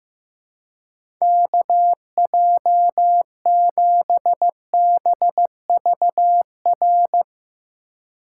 Morse code, heard as a keyed tone, about 15 wpm, 700 Hz. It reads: KJ7BVR
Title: USA Call Signs - 6 Characters - 4x Character Spacing - 15wpm